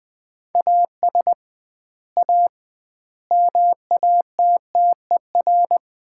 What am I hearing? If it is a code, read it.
Morse code, heard as a keyed tone, about 21 wpm, 700 Hz. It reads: AS A MATTER
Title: 2x Word-Spacing - Sets of 5 Words - 20wpm